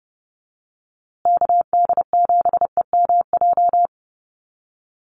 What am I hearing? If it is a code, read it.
Morse code, heard as a keyed tone, about 30 wpm, 700 Hz. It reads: KD7EMJ
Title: USA Call Signs Repeated 3x 30wpm